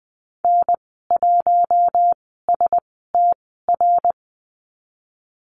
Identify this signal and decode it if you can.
Morse code, heard as a keyed tone, about 20 wpm, spaced as for 15 wpm, 700 Hz. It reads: N1STR
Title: 055 Lesson 10g - Letter r - Call Signs 20wpm